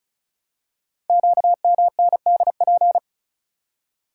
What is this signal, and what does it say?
Morse code, heard as a keyed tone, about 36 wpm, 700 Hz. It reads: QMNDP
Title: Code Groups - 5 Letters - 35wpm